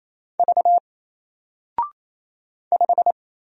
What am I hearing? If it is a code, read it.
Morse code, heard as a keyed tone, about 27 wpm, 700 Hz. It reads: V 5
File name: Single-Letter-Number-28wpm